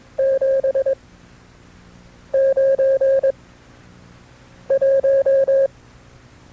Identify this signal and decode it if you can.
Morse code, heard as a keyed tone, about 22 wpm, spaced as for 6 wpm, 540 Hz. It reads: 791
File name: FreeFiles / CW-practice / KB6MT-tape2-side2